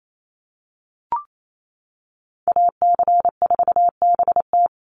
Morse code, heard as a keyed tone, about 28 wpm, 700 Hz. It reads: AC4BT
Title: Call Signs - 28wpm